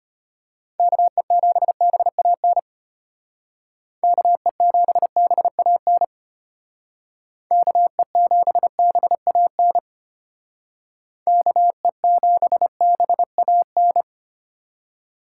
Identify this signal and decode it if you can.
Morse code, heard as a keyed tone, about 32 wpm, 700 Hz. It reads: KE7BAN KE7BAN KE7BAN KE7BAN